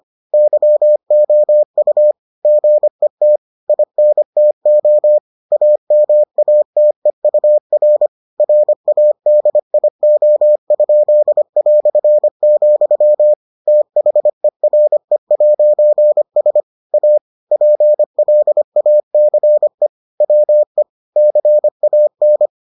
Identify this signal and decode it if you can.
Morse code, heard as a keyed tone, about 25 wpm, 600 Hz. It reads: YOU GET INTO AMATEUR RADIO?", THERE'S A PLACE WE CAN